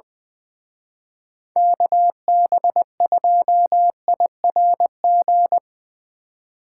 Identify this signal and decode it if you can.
Morse code, heard as a keyed tone, about 22 wpm, 700 Hz. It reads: KB2IRG
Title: Speed Racing - USA 2x3 Call Signs 20wpm